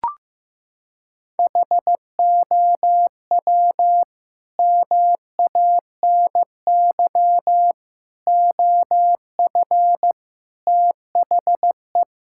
Morse code, heard as a keyed tone, about 15 wpm, 700 Hz. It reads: HOW MANY OF THE
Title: Sentences from Top 1000 Words - Review Part 6 - 15wpm